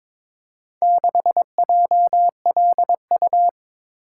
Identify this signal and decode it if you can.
Morse code, heard as a keyed tone, about 22 wpm, 700 Hz. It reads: 6JLU
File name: practice / 22wpm / Four Letter-Number Combinations - 22wpm